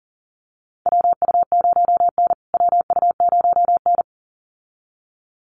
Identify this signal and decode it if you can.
Morse code, heard as a keyed tone, about 40 wpm, 700 Hz. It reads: WU0N WU0N
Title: USA Call Signs Repeated 2x 40wpm